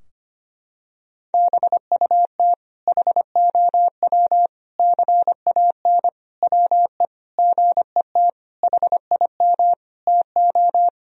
Morse code, heard as a keyed tone, about 25 wpm, 700 Hz. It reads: BUT HOW CAN WE GET HIM TO
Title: Sentences from Top 1000 Words - Review Part 4 - 25wpm